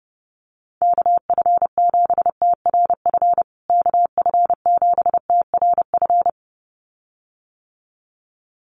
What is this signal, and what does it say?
Morse code, heard as a keyed tone, about 30 wpm, 700 Hz. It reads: KF7TRF KF7TRF
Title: USA Call Signs Repeated 2x 30wpm